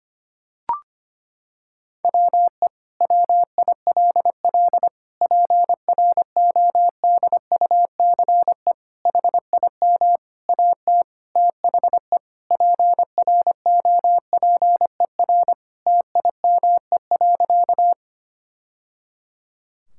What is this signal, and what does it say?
Morse code, heard as a keyed tone, about 25 wpm, 700 Hz. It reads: WE WILL PRODUCE HIM AT THE PROPER TIME.
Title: Sentences from Top 1000 Words - Review Part 5 - 25wpm